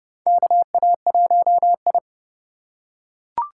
Morse code, heard as a keyed tone, about 29 wpm, 700 Hz. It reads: KA1I
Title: USA Call Signs - 4 Characters - 4x Character Spacing - 30wpm